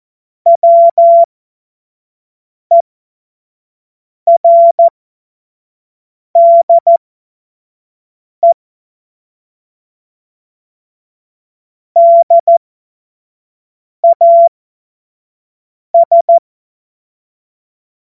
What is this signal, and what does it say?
Morse code, heard as a keyed tone, about 14 wpm, spaced as for 5 wpm, 680 Hz. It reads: WERDE DAS
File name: Morsekurs / Witze / Witz 23. August 2022 25 F70